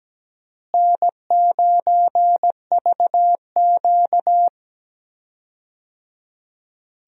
Morse code, heard as a keyed tone, about 17 wpm, 700 Hz. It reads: N9VQ
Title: USA Call Signs - 4 Characters - Repeated 3x - 17wpm